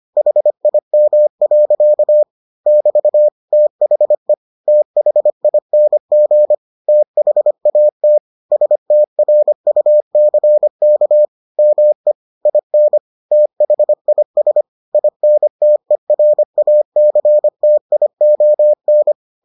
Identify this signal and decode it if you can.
Morse code, heard as a keyed tone, about 25 wpm, 600 Hz. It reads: HIM. <BT> THE THING THAT STRUCK ME IN THIS INTERACTION